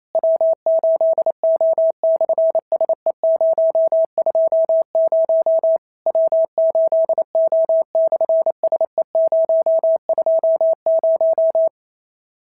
Morse code, heard as a keyed tone, about 28 wpm, 650 Hz. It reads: W8O/SE020 W8O/SE020
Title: SOTA QSO Practice 28wpm